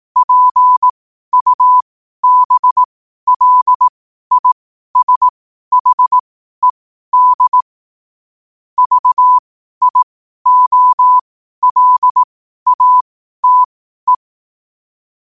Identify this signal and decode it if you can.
Morse code, heard as a keyed tone, about 18 wpm, spaced as for 12 wpm, 1000 Hz. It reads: PUBLISHED VIOLATE
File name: qotdfiles / 12wpm